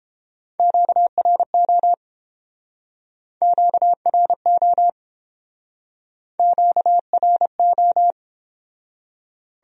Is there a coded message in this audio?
Yes: Morse code from 0.6 to 8.1 s, heard as a keyed tone, about 30 wpm, 700 Hz. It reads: QRO QRO QRO